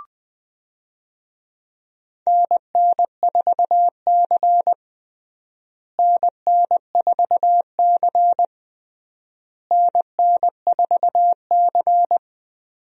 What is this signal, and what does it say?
Morse code, heard as a keyed tone, about 20 wpm, 700 Hz. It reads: NN4C NN4C NN4C
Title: USA Call Signs - 4 Characters - Repeated 3x - 20wpm